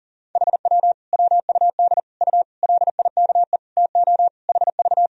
Morse code, heard as a keyed tone, about 37 wpm, 700 Hz. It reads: HW WUD U LIKE TO HV
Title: Sentences from Top 100 Words and Abbreviations 40wpm